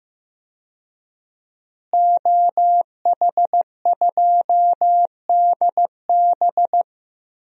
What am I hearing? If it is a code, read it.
Morse code, heard as a keyed tone, about 15 wpm, 700 Hz. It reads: OH2DB
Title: DXCC-100 Call Signs - 5 Characters - Repeated 2x - 15wpm